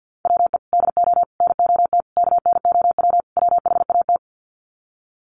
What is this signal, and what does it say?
Morse code, heard as a keyed tone, about 50 wpm, 700 Hz. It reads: WE DO NOT KNOW WHAT
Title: Sets of 5 Words 50wpm